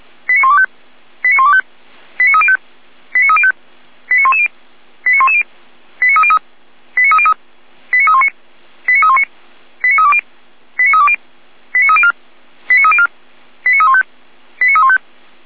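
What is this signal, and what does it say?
Five-tone selective call (ZVEI), heard as quick runs of five tones, tones of about 70 ms. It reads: 89125 89125 89305 89305 89110 89110 89303 89303 89319 89319 89320 89320 89304 89304 89315 89315
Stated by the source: ZVEI five-tone call received on an SDR, channel 455.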